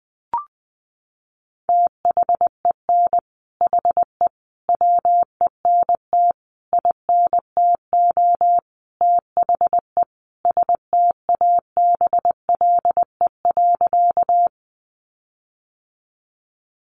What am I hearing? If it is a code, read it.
Morse code, heard as a keyed tone, about 20 wpm, 700 Hz. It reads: THEN HE WENT INTO THE STABLE.
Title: Sentences from Top 1500 Words 20wpm